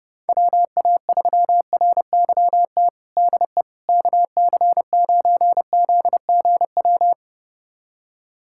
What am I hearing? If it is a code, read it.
Morse code, heard as a keyed tone, about 30 wpm, 700 Hz. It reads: WA3RYT DE KC9ZGW
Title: USA Callsign DE Callsign 30wpm